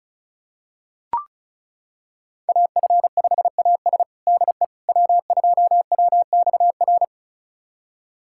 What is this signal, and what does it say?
Morse code, heard as a keyed tone, about 36 wpm, 700 Hz. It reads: AF5AS DE W2WXR
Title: USA Callsign DE Callsign - 5 Characters - 35wpm